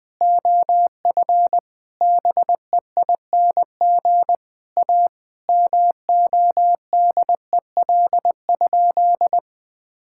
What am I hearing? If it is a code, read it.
Morse code, heard as a keyed tone, about 20 wpm, 700 Hz. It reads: OF BEING A MODEL?
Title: Sentences from Top 1000 Words - Review Part 5 - 20wpm